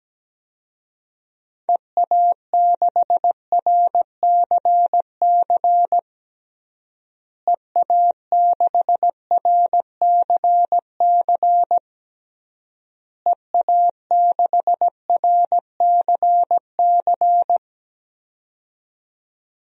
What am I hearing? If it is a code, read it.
Morse code, heard as a keyed tone, about 17 wpm, 700 Hz. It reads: EA6RCC EA6RCC EA6RCC
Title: DXCC-100 Call Signs - 6 Characters - Repeated 3x - 17wpm